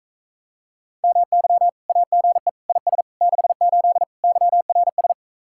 Morse code, heard as a keyed tone, about 45 wpm, 700 Hz. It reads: MY AGE IS 68 YRS